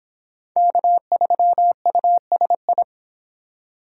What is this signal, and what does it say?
Morse code, heard as a keyed tone, about 25 wpm, 700 Hz. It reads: K3USI